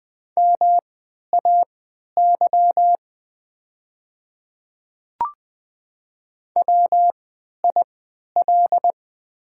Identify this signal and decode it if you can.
Morse code, heard as a keyed tone, about 20 wpm, spaced as for 11 wpm, 690 Hz. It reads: MAY WIL
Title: 3x Character Spacing - Top 200 Words - 20wpm